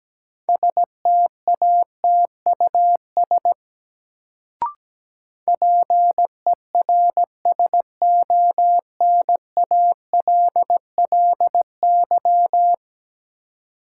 Morse code, heard as a keyed tone, about 17 wpm, 700 Hz. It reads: STATUS PERSONALLY
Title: Sentences from Top 1400 Words - New Word Drill - 17wpm